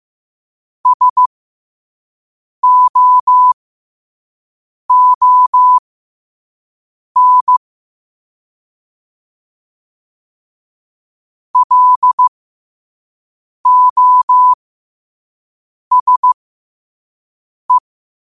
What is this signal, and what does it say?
Morse code, heard as a keyed tone, about 15 wpm, spaced as for 5 wpm, 1000 Hz. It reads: SOON LOSE
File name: qotdfiles / 5wpm